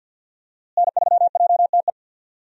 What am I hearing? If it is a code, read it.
Morse code, heard as a keyed tone, about 47 wpm, 700 Hz. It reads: N3JTE
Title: USA Call Signs - 5 Characters - Repeated 3x - 50wpm